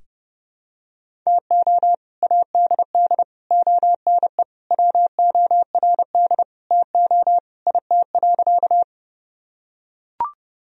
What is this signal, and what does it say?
Morse code, heard as a keyed tone, about 30 wpm, 700 Hz. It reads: TO ADD ONE WORD TO IT.
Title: Sentences from Top 1000 Words - Review Part 1 - 30wpm